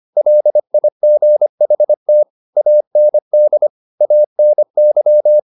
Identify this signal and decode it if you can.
Morse code, heard as a keyed tone, about 25 wpm, 600 Hz. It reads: LIGHT AND ANY
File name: the old man and the sea part 1 25wpm